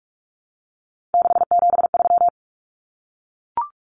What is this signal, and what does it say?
Morse code, heard as a keyed tone, about 44 wpm, 700 Hz. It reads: 673
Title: Common Cut Numbers - 3-Digit 45wpm